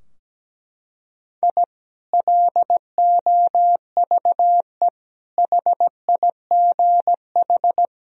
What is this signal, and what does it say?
Morse code, heard as a keyed tone, about 17 wpm, 700 Hz. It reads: I LOVE HIGH
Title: Sentences from Top 1000 Words - Review Part 4 - 17wpm